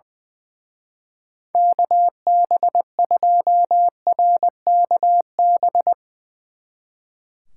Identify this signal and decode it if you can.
Morse code, heard as a keyed tone, about 21 wpm, 700 Hz. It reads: KB2RKB
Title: Speed Racing - USA 2x3 Call Signs 20wpm